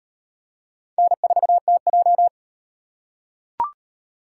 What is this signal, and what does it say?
Morse code, heard as a keyed tone, about 38 wpm, 700 Hz. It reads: N4TJ